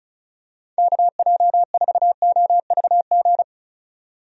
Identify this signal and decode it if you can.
Morse code, heard as a keyed tone, about 36 wpm, 700 Hz. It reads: KJ4OVG
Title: USA Call Signs - 6 Characters - 35wpm